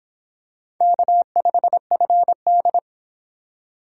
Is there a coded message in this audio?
Yes: Morse code from 0.8 to 2.8 s, heard as a keyed tone, about 23 wpm, 700 Hz. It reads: K5FD